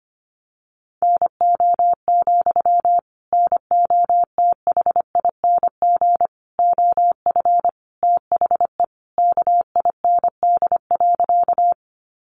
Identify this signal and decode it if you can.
Morse code, heard as a keyed tone, about 25 wpm, 700 Hz. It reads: NO, NOTHING OF THE KIND.
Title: Sentences from Top 500 Words 25wpm